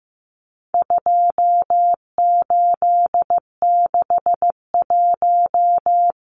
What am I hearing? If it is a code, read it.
Morse code, heard as a keyed tone, about 15 wpm, 700 Hz. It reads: 2861
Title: Numbers - 4-Digit 15wpm